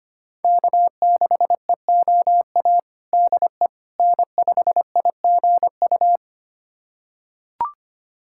Morse code, heard as a keyed tone, about 25 wpm, 700 Hz. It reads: K6EOA DE N5IGU
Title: USA Callsign DE Callsign - 5 Characters - 25wpm